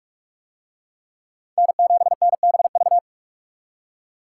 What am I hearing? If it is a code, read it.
Morse code, heard as a keyed tone, about 43 wpm, 700 Hz. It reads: N7NBV